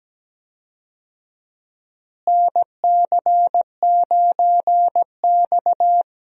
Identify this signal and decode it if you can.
Morse code, heard as a keyed tone, about 17 wpm, 700 Hz. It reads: NC9X
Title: USA Call Signs - 4 Characters - Repeated 3x - 17wpm